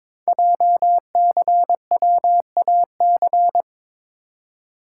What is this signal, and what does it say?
Morse code, heard as a keyed tone, about 22 wpm, 700 Hz. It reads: JCWAC
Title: Code Groups - 5 Letters - 22wpm